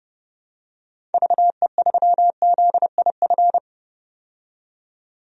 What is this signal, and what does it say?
Morse code, heard as a keyed tone, about 31 wpm, 700 Hz. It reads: VE3ZIF
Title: DXCC-100 Call Signs - 6 Characters - 30wpm